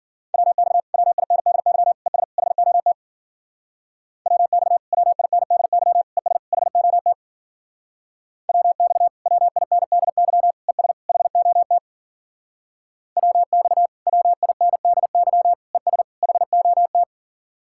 Morse code, heard as a keyed tone, about 51 wpm, 700 Hz. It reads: WX WINDY ES HOT WX WINDY ES HOT WX WINDY ES HOT WX WINDY ES HOT